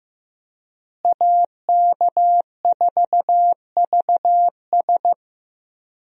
Morse code, heard as a keyed tone, about 15 wpm, 700 Hz. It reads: AK4VS
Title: USA Call Signs - 5 Characters - Repeated 3x - 15wpm